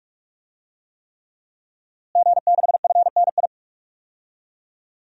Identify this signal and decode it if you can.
Morse code, heard as a keyed tone, about 45 wpm, 700 Hz. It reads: G6FNI
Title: DXCC-100 Call Signs - 5 Characters - Repeated 3x - 45wpm